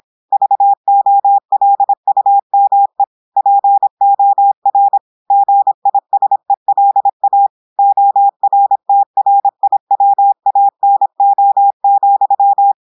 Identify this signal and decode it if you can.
Morse code, heard as a keyed tone, about 26 wpm, 800 Hz. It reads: VOLUME POR GISELA ORTRIWANO,
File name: audios / portugais / portugais texte en clair 1 vitesse 26 mots minutes